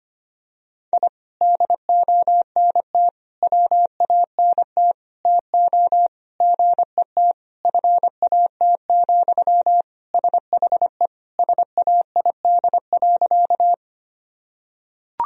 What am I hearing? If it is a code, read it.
Morse code, heard as a keyed tone, about 25 wpm, 700 Hz. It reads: I DONT WANT TO GET FAT, SHE SAID.
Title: Sentences from Top 1000 Words - Review Part 4 - 25wpm